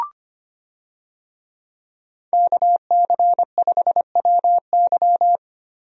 Morse code, heard as a keyed tone, about 25 wpm, 700 Hz. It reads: KC5WY
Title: USA Call Signs - 5 Characters - Repeated 3x - 25wpm